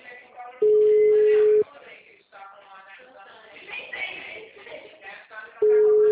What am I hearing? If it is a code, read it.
Ringback tone (Europe, 425 Hz), with speech.
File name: out- -2016- - - .1573